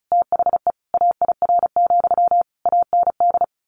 Morse code, heard as a keyed tone, about 34 wpm, 700 Hz. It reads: THE AIR, AND
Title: A Sound Of Thunder 35wpm